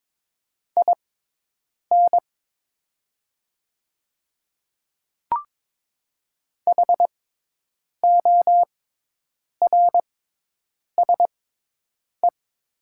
Morse code, heard as a keyed tone, about 22 wpm, spaced as for 8 wpm, 700 Hz. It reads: IN HORSE
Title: 6x Character Spacing - Top 300 Words - 22wpm